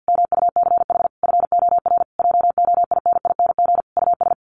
Morse code, heard as a keyed tone, about 51 wpm, 700 Hz. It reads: MUCH FOR JOINING US
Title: Sets of 5 Words 50wpm